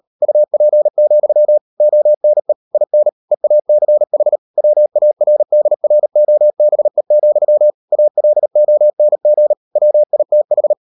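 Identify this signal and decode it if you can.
Morse code, heard as a keyed tone, about 38 wpm, 600 Hz. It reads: UP, ONE IN EACH WARDROBE, ALONG WITH
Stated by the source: Morse code practice at 38 WPM, sending Ken Follett's novel Never in English, from CW med Gnister.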